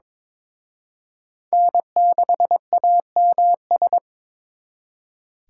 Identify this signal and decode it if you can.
Morse code, heard as a keyed tone, about 26 wpm, 700 Hz. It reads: N6AMS
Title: Speed Racing - USA 1x3 Call Signs 22wpm